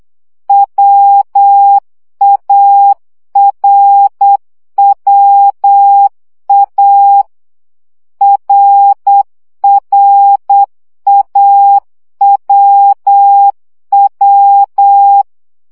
Morse code, heard as a keyed tone, about 8 wpm, 800 Hz. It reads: WARWA RRAWW
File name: CW gyakorlas a r w 7 baud